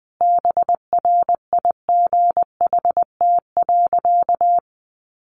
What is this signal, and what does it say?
Morse code, heard as a keyed tone, about 20 wpm, 700 Hz. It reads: BRIGHT.
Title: Sentences from Top 1500 Words 20wpm